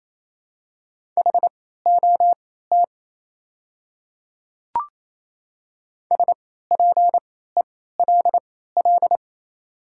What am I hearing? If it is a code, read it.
Morse code, heard as a keyed tone, about 27 wpm, spaced as for 16 wpm, 690 Hz. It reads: HOT SPELL
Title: 3x Character Spacing - Top 200 Words - 28wpm